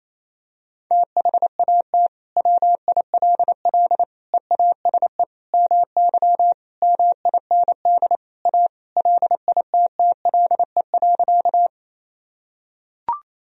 Morse code, heard as a keyed tone, about 28 wpm, 700 Hz. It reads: THAT WILL EASE MY MIND A LITTLE.
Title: Sentences from Top 1000 Words - Review Part 1 - 28wpm